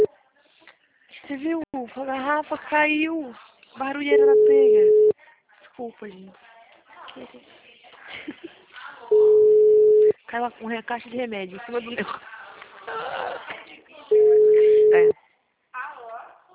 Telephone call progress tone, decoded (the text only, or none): ringback tone